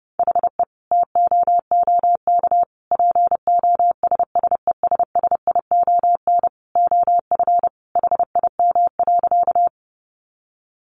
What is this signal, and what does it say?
Morse code, heard as a keyed tone, about 30 wpm, 700 Hz. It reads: HE TOOK POSSESSION OF HIM.
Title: Sentences from Top 1400 Words 30wpm